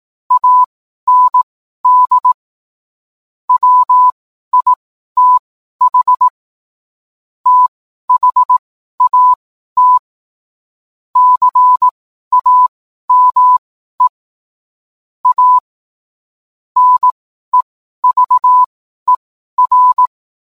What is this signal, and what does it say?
Morse code, heard as a keyed tone, about 18 wpm, spaced as for 12 wpm, 1000 Hz. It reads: AND WITH THAT CAME A NEVER